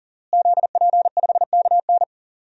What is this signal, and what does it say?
Morse code, heard as a keyed tone, about 37 wpm, 700 Hz. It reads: ZP5KN
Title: DXCC-100 Call Signs - 5 Characters - Repeated 2x - 40wpm